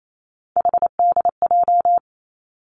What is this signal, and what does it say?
Morse code, heard as a keyed tone, about 28 wpm, 700 Hz. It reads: HDJ
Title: Three Challenging Letter-Number Combinations 28wpm